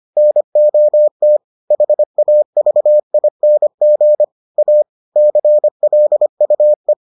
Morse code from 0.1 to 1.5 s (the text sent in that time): NOT